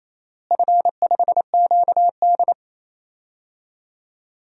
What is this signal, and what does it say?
Morse code, heard as a keyed tone, about 28 wpm, 700 Hz. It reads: F5QD
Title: DXCC-100 Call Signs - 4 Characters - Repeated 3x - 28wpm